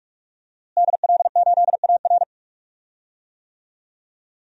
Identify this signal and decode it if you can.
Morse code, heard as a keyed tone, about 46 wpm, 700 Hz. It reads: DL8AR
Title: DXCC-100 Call Signs - 5 Characters - Repeated 3x - 45wpm